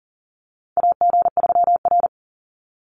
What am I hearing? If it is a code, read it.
Morse code, heard as a keyed tone, about 40 wpm, 700 Hz. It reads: AG3R
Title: Call Signs Repeated 3x - 40wpm